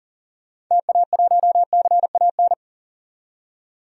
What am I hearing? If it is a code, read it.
Morse code, heard as a keyed tone, about 40 wpm, 700 Hz. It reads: TA1CAN
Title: DXCC-100 Call Signs - 6 Characters - 40wpm